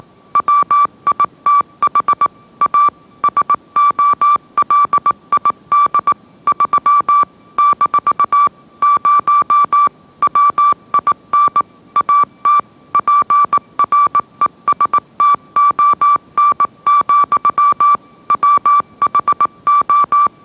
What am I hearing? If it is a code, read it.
Morse code, heard as a keyed tone, about 18 wpm, 1160 Hz. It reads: WITH A SOLID 3 - 0 WIN AT PRESTON, WHO
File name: mrx / mrx081 20wpm